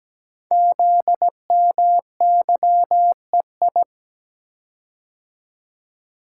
Morse code, heard as a keyed tone, about 17 wpm, 700 Hz. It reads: ZMYEI
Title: Code Groups - 5 Letters - 17wpm